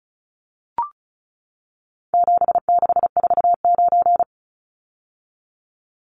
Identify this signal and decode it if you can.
Morse code, heard as a keyed tone, about 34 wpm, 700 Hz. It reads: 7649
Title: Numbers - 4-Digit 35wpm